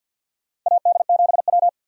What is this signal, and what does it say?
Morse code, heard as a keyed tone, about 47 wpm, 700 Hz. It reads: AD7W
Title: USA Call Signs - 4 Characters - 50wpm